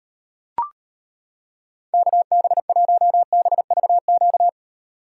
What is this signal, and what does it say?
Morse code, heard as a keyed tone, about 32 wpm, 700 Hz. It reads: KB1BVQ